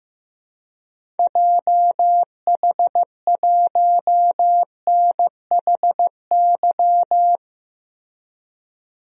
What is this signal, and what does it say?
Morse code, heard as a keyed tone, about 15 wpm, 700 Hz. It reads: JH1NHY
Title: DXCC-100 Call Signs - 6 Characters - 15wpm